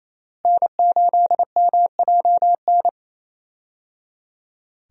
Morse code, heard as a keyed tone, about 27 wpm, 700 Hz. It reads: N8MJN
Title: USA Call Signs - 5 Characters - Repeated 3x - 28wpm